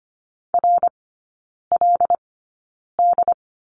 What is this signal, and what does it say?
Morse code, heard as a keyed tone, about 25 wpm, spaced as for 9 wpm, 700 Hz. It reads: RLD
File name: Lesson 14e - Number 2 - Three Characters 25wpm